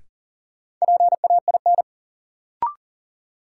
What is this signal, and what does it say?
Morse code, heard as a keyed tone, about 40 wpm, 700 Hz. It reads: PAIN
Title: Sentences from Top 1600 Words - New Word Drill Alt - 40wpm